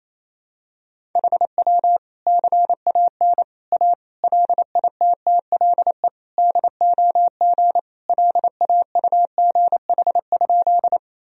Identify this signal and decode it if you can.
Morse code, heard as a keyed tone, about 28 wpm, 700 Hz. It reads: HW CAN A LITTLE DOG LAUGH?
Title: Sentences from Top 500 Words and Abbreviations 28wpm